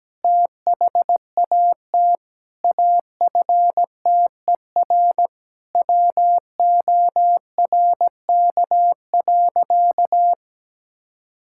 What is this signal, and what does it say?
Morse code, heard as a keyed tone, about 17 wpm, 700 Hz. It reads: THAT AFTER WORK.
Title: Sentences from Top 1000 Words - Review Part 5 - 17wpm